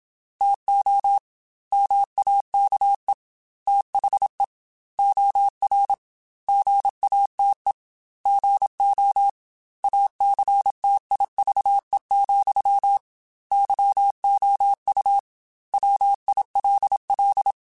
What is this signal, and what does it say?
Morse code, heard as a keyed tone, about 27 wpm, 800 Hz. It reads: TO MAKE THE OR GATE GO ACTIVE, YOU WILL
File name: Technical Subjects / Technical Subjects-mp3s / OR Gate @25